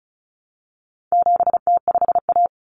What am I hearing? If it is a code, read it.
Morse code, heard as a keyed tone, about 34 wpm, 700 Hz. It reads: 7T5A
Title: Common Cut Numbers - 4-Digit 35wpm